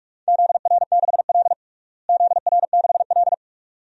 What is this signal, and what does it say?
Morse code, heard as a keyed tone, about 47 wpm, 700 Hz. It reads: ZR6L ZR6L
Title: DXCC-100 Call Signs - 4 Characters - Repeated 2x - 45wpm